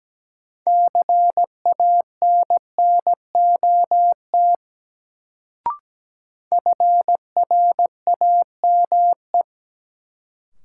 Morse code, heard as a keyed tone, about 17 wpm, 700 Hz. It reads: CANNOT FRAME
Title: Sentences from Top 1000 Words - Review Part 6 - New Word Drill - 17wpm